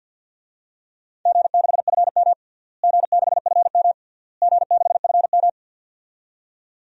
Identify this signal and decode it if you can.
Morse code, heard as a keyed tone, about 52 wpm, 700 Hz. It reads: G6FM G6FM G6FM
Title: DXCC-100 Call Signs - 4 Characters - Repeated 3x - 50wpm